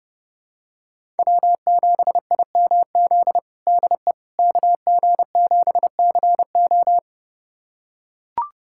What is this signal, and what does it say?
Morse code, heard as a keyed tone, about 30 wpm, 700 Hz. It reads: W7IMZ DE KG7CO
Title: USA Callsign DE Callsign - 5 Characters - 30wpm